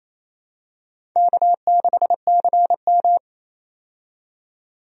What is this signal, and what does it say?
Morse code, heard as a keyed tone, about 28 wpm, 700 Hz. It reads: K6CM
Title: USA Call Signs - 4 Characters - 28wpm